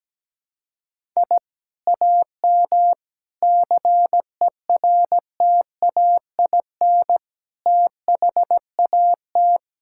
Morse code, heard as a keyed tone, about 17 wpm, 700 Hz. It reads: I AM CERTAIN THAT
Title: Sentences from Top 1000 Words - Review Part 6 - 17wpm